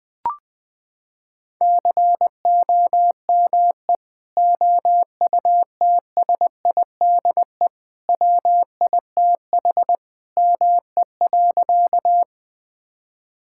Morse code, heard as a keyed tone, about 20 wpm, 700 Hz. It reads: COME OUTSIDE WITH ME.
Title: Sentences from Top 1000 Words - Review Part 6 - 20wpm